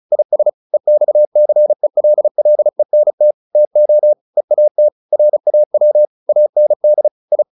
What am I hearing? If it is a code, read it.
Morse code, heard as a keyed tone, about 34 wpm, 600 Hz. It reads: IS EXCELLENT TO EAT RAW AND I
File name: the old man and the sea / the old man and the sea 1 35wpm